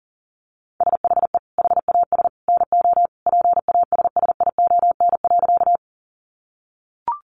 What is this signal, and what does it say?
Morse code, heard as a keyed tone, about 37 wpm, 700 Hz. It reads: SHE HAS NO PASSION.
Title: Sentences from Top 1300 Words 40wpm